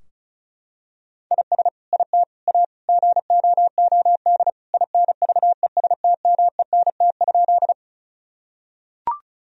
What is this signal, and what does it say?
Morse code, heard as a keyed tone, about 34 wpm, 700 Hz. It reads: IS IT A GOOD INVESTMENT?
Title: Sentences from Top 1000 Words - Review Part 4 - 35wpm